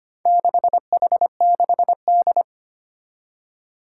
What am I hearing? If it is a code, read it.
Morse code, heard as a keyed tone, about 25 wpm, 700 Hz. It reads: 6H6D
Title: Four Challenging Letter-Number Combinations 25wpm